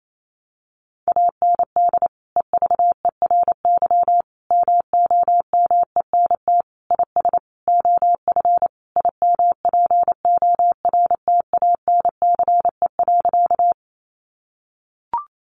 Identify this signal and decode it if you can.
Morse code, heard as a keyed tone, about 28 wpm, 700 Hz. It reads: AND EVERY MOMENT IS OF IMPORTANCE.
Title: Sentences from Top 1200 Words 28wpm